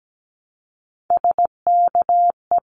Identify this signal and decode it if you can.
Morse code, heard as a keyed tone, about 17 wpm, 700 Hz. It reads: SKE
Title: Abbreviations Repeated 3x - 17wpm